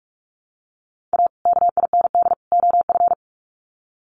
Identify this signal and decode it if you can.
Morse code, heard as a keyed tone, about 44 wpm, 700 Hz. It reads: A KIND OF
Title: Sets of 3 Words 45wpm